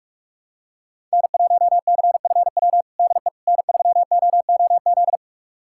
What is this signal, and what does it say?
Morse code, heard as a keyed tone, about 42 wpm, 700 Hz. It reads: N1CFW DE N3OOZ
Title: USA Callsign DE Callsign 45wpm